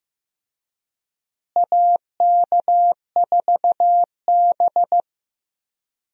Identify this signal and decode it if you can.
Morse code, heard as a keyed tone, about 15 wpm, 700 Hz. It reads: AK4B
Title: USA Call Signs - 4 Characters - Repeated 3x - 15wpm